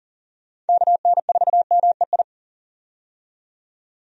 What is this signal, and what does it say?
Morse code, heard as a keyed tone, about 40 wpm, 700 Hz. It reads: KN4MEI
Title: USA Call Signs - 6 Characters - 40wpm